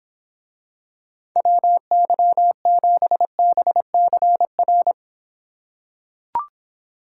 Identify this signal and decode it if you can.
Morse code, heard as a keyed tone, about 24 wpm, 700 Hz. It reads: WY7BCR